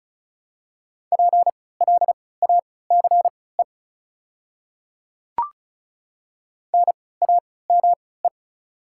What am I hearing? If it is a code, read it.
Morse code, heard as a keyed tone, about 36 wpm, spaced as for 20 wpm, 700 Hz. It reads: PLACE NAME
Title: 3x Character Spacing - Top 300 Words - 35wpm